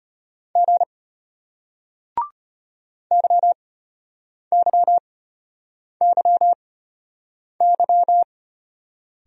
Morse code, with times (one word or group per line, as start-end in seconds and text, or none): 0.6-0.8 G
3.1-8.2 YYYY